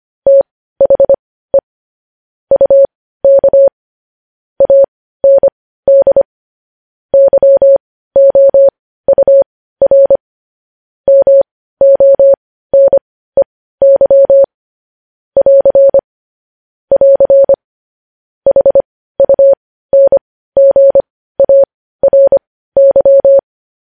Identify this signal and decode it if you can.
Morse code, heard as a keyed tone, about 25 wpm, spaced as for 15 wpm, 550 Hz. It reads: THE UK AND YOUR MONEY <AR> <AR> HUNGARY
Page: news headlines in morse code at 15 wpm